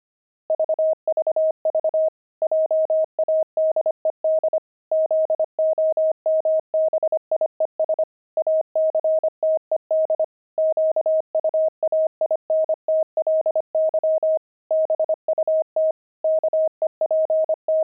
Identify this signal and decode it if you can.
Morse code, heard as a keyed tone, about 25 wpm, 630 Hz. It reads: VVV JADED ZOMBIES ACTED QUAINTLY BUT KEPT
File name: pangram-fs-25-cs-25-jaded